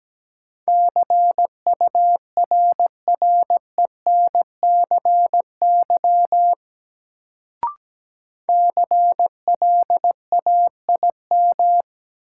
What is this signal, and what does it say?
Morse code, heard as a keyed tone, about 17 wpm, 700 Hz. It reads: CURRENCY CLAIM
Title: Sentences from Top 1500 Words - New Word Drill Alt - 17wpm